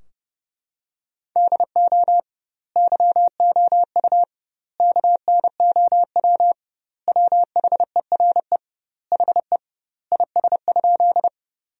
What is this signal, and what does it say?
Morse code, heard as a keyed tone, about 30 wpm, 700 Hz. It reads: DO YOU KNOW WHERE HE IS?
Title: 2x Word-Spacing - Sentences from Top 300 Words - 30wpm